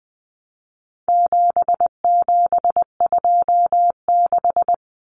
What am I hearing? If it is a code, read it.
Morse code, heard as a keyed tone, about 20 wpm, 700 Hz. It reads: 7726
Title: Numbers - 4-Digit 20wpm